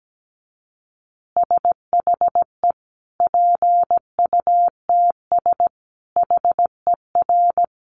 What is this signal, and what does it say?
Morse code, heard as a keyed tone, about 17 wpm, 700 Hz. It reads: SHE PUTS HER
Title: Easy Phrases 17wpm